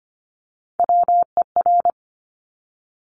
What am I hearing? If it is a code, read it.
Morse code, heard as a keyed tone, about 25 wpm, 700 Hz. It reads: WER
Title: Top Three Letter Combinations in Words 25wpm